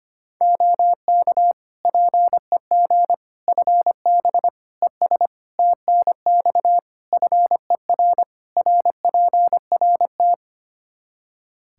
Morse code, heard as a keyed tone, about 25 wpm, 700 Hz. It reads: OK PEG FB ES TNX FER RPRT
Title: Practice QSO - Exchange 2 - Info - 25wpm